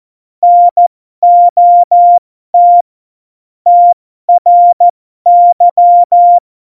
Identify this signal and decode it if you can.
Morse code, heard as a keyed tone, about 14 wpm, spaced as for 12 wpm, 700 Hz. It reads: NOT TRY
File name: Morsekurs / Witze